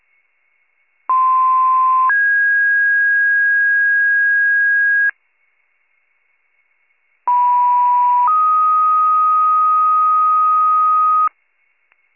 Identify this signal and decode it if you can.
Two-tone page, heard as a one-second tone, then a three-second tone, about 1040 then 1650 Hz, about 980 then 1250 Hz. It is heard from 1.1 to 11.3 s.